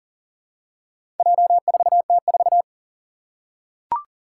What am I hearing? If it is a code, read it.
Morse code, heard as a keyed tone, about 42 wpm, 700 Hz. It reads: J4T4
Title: Four Challenging Letter-Number Combinations 40wpm